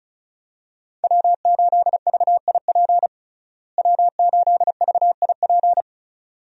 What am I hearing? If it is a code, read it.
Morse code, heard as a keyed tone, about 34 wpm, 700 Hz. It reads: W8VIP W8VIP
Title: USA Call Signs - 5 Characters - Repeated 2x - 35wpm